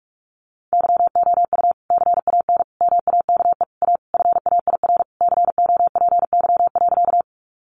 Morse code, heard as a keyed tone, about 46 wpm, 700 Hz. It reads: YOU CAN MAKE A FAIR COPY.
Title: Sentences from Top 1400 Words 45wpm